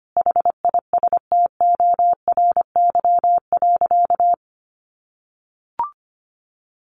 Morse code, heard as a keyed tone, about 25 wpm, 700 Hz. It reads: HISTORY.
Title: Sentences from Top 1300 Words 25wpm